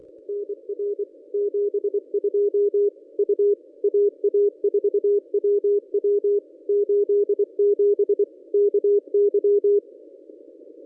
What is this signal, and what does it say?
Morse code, heard as a keyed tone, about 24 wpm, 410 Hz. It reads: NR 72 U AA4WW 87 KY
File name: contests / ss-cw-k5tr-op-n5rz -1000